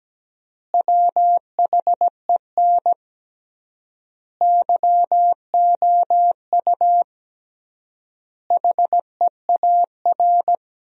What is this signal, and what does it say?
Morse code, heard as a keyed tone, about 17 wpm, 700 Hz. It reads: WHEN YOU HEAR
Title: 3x Word-Spacing - Sentences from Top 300 Words - 17wpm